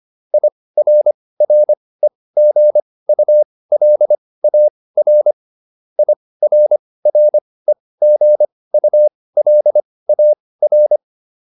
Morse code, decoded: IRREGULAR IRREGULAR

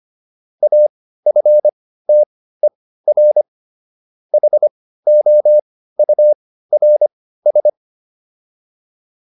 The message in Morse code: AFTER HOURS

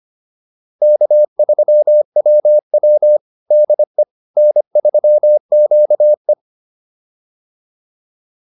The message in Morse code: K3WW DE N3QE

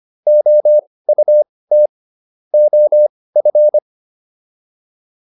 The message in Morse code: OUT OF

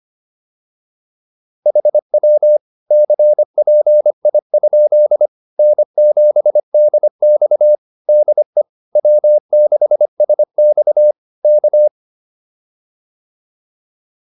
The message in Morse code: HW CPI? N7DX DE W6SX K

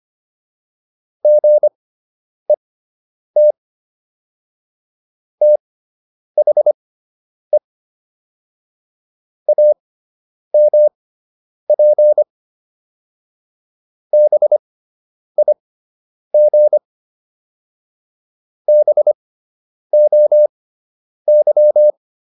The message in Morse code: GET THE AMP BIG BOY

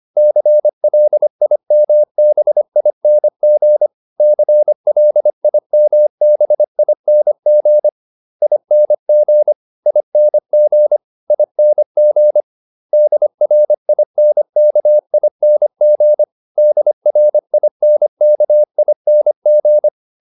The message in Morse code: CLIMBING CLIMBING ING ING ING DRINKING DRINKING

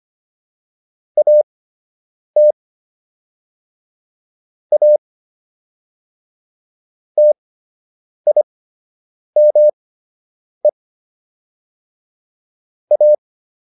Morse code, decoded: AT A TIME A

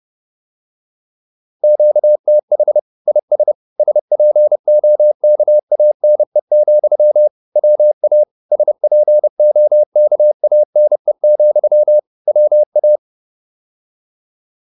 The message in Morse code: QTH IS SPOKANE, WA SPOKANE, WA

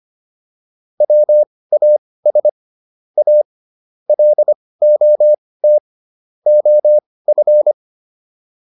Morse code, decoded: WAS A LOT OF

